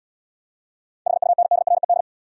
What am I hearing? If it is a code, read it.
Morse code, heard as a keyed tone, about 78 wpm, 700 Hz. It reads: HUNDRED